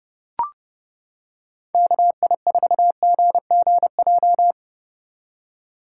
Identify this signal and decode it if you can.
Morse code, heard as a keyed tone, about 31 wpm, 700 Hz. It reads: KI4GGJ